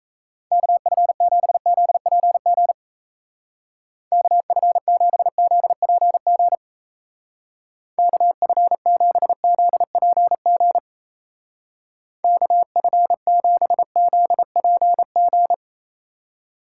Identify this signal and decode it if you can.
Morse code, heard as a keyed tone, about 35 wpm, 700 Hz. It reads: KF7ZPG KF7ZPG KF7ZPG KF7ZPG
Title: Speed Racing - USA 2x3 Call Signs 28wpm